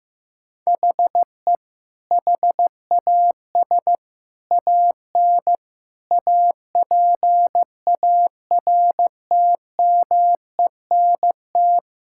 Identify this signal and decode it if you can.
Morse code, heard as a keyed tone, about 15 wpm, 700 Hz. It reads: HE HAS AN APARTMENT
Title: Sentences from Top 1000 Words - Review Part 6 - 15wpm